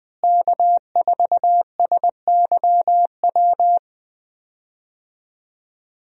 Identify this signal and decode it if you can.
Morse code, heard as a keyed tone, about 20 wpm, 700 Hz. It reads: K4SYW
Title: USA Call Signs - 5 Characters - Repeated 3x - 20wpm